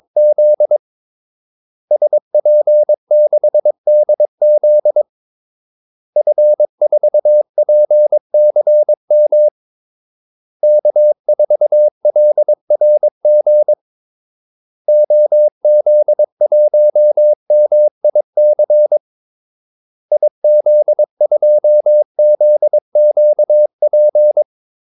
Morse code, heard as a keyed tone, about 22 wpm, 600 Hz. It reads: Z SP6DZ F4PCM K4LRG OZ1MIC IZ2ZQP
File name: Callsigns-22wpm